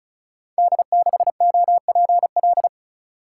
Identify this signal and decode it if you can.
Morse code, heard as a keyed tone, about 36 wpm, 700 Hz. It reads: D6OPL
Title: Code Groups - 5 Letter:Numbers - 35wpm